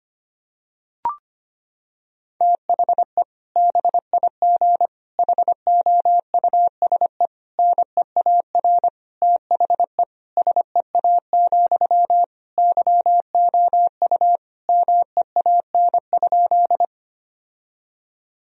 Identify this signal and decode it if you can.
Morse code, heard as a keyed tone, about 25 wpm, 700 Hz. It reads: THE BIG HOUSE NEAR THE SEA, YOU MEAN?
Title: Sentences from Top 1000 Words - Review Part 1 - 25wpm